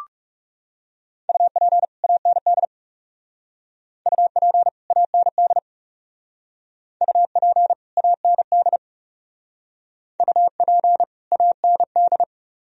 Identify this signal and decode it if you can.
Morse code, heard as a keyed tone, about 37 wpm, 700 Hz. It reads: UP AND UP AND UP AND UP AND
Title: Speed-Racing - Sets of 2 Words 30wpm